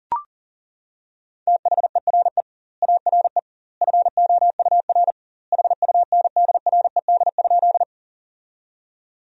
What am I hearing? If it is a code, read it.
Morse code, heard as a keyed tone, about 40 wpm, 700 Hz. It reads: THERE ARE FOUR HUNDRED?
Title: 2x Word-Spacing - Sentences from Top 400 Words - 40wpm